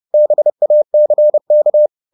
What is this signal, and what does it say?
Morse code, heard as a keyed tone, about 30 wpm, 600 Hz. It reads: BACK
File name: the old man and the sea 2 30wpm